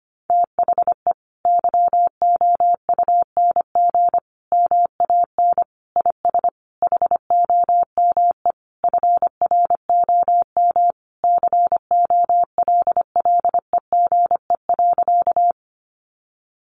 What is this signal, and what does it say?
Morse code, heard as a keyed tone, about 25 wpm, 700 Hz. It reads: THE YOUNG MAN IS HOME FROM COLLEGE.
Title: Sentences from Top 1300 Words 25wpm